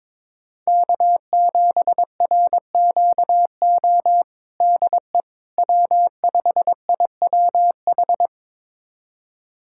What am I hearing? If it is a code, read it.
Morse code, heard as a keyed tone, about 22 wpm, 700 Hz. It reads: K7RQO DE W5IWH
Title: USA Callsign DE Callsign - 5 Characters - 22wpm